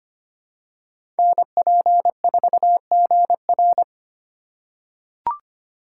Morse code, heard as a keyed tone, about 25 wpm, 700 Hz. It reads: NP4GR
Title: USA Call Signs - 5 Characters - 4x Character Spacing - 25wpm